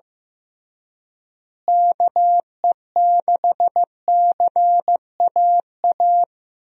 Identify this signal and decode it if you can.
Morse code, heard as a keyed tone, about 15 wpm, 700 Hz. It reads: KE6CAA
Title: USA Call Signs - 6 Characters - Repeated 3x - 15wpm